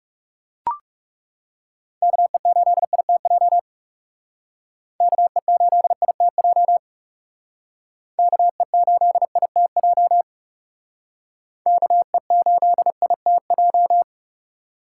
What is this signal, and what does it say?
Morse code, heard as a keyed tone, about 38 wpm, 700 Hz. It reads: KE8ITJ KE8ITJ KE8ITJ KE8ITJ